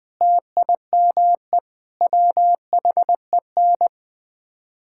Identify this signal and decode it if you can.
Morse code, heard as a keyed tone, about 20 wpm, 700 Hz. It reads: TIME WHEN
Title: Build A Phrase - 5-Word Phrases - 20wpm